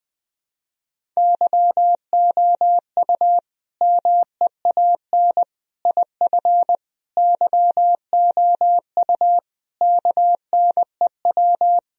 Morse code, heard as a keyed tone, about 21 wpm, 700 Hz. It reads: YOU MEAN IF YOU KNEW
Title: Sentences from Top 1000 Words - Review Part 4 - 20wpm